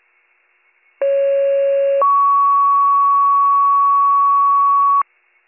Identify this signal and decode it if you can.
Two-tone page, heard as a one-second tone, then a three-second tone, about 550 then 1120 Hz. It is heard from 1.0 to 5.0 s.